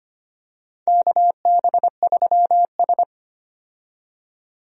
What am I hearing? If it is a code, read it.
Morse code, heard as a keyed tone, about 25 wpm, 700 Hz. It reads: KB3S
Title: USA Call Signs - 4 Characters - 25wpm